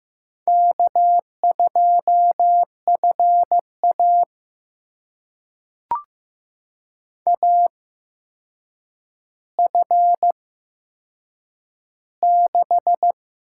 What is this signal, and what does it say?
Morse code, heard as a keyed tone, about 15 wpm, spaced as for 7 wpm, 700 Hz. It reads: K2FA AF6
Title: USA Call Signs - 8x Character Spacing - 15wpm